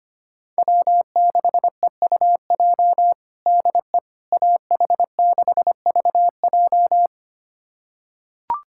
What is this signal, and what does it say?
Morse code, heard as a keyed tone, about 25 wpm, 700 Hz. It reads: W6EUJ DE AH6VJ
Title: USA Callsign DE Callsign - 5 Characters - 25wpm